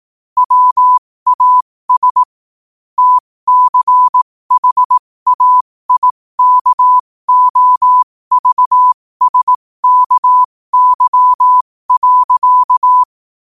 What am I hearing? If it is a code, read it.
Morse code, heard as a keyed tone, about 18 wpm, spaced as for 15 wpm, 1000 Hz. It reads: WAS TCHAIKOVSKY.
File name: qotdfiles / 15wpm